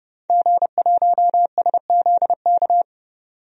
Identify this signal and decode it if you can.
Morse code, heard as a keyed tone, about 30 wpm, 700 Hz. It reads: G1SZK
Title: DXCC-100 Call Signs - 5 Characters - 30wpm